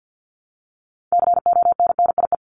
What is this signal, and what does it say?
Morse code, heard as a keyed tone, about 50 wpm, 700 Hz. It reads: CONNIE
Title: CWT Exchange 50wpm